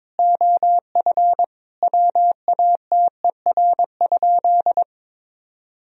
Morse code, heard as a keyed tone, about 22 wpm, 700 Hz. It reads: OF WATER?
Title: Sentences from Top 1000 Words - Review Part 3 - 22wpm